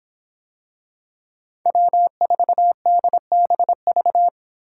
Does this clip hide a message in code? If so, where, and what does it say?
Yes: Morse code from 1.7 to 4.3 s, heard as a keyed tone, about 25 wpm, 700 Hz. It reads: W4DBV